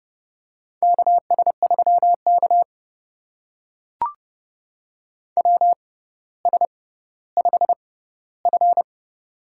Morse code, heard as a keyed tone, about 32 wpm, spaced as for 16 wpm, 700 Hz. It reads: KS3K WS5F
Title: USA Call Signs - 4 Characters - 6x Character Spacing - 30wpm